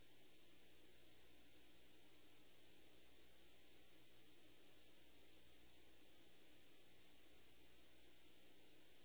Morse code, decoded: none

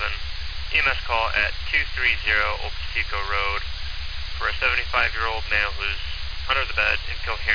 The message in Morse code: none

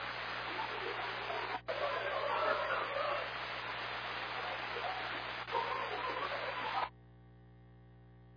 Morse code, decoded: none

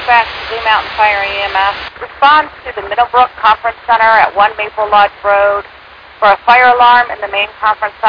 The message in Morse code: none

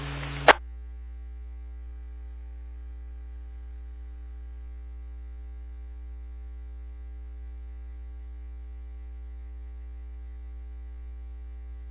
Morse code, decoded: none

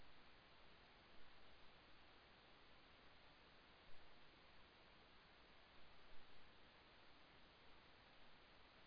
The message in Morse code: none

